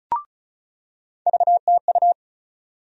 Morse code, heard as a keyed tone, about 34 wpm, 700 Hz. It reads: VTU
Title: USA Call Sign Postfixes - 3 Letters 35wpm